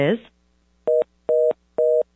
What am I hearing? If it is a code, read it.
Reorder tone (North America), 480 + 620 Hz, with speech.
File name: 575-385-9999 valor